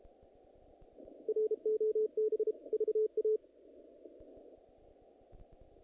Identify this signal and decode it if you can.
Morse code, heard as a keyed tone, about 32 wpm, 430 Hz. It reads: ROB VA
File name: contests / naqp-cw-k5tr-op-k5pi- -2000